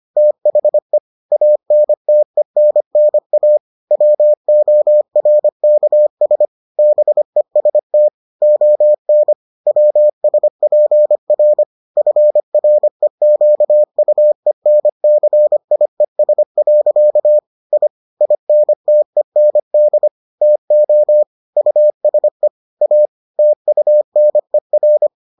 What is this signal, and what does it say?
Morse code, heard as a keyed tone, about 25 wpm, 600 Hz. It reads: THE ANTENNA WORKS BEST ON WSPR FREQUENCIES. I INTEND TO USE A TUNER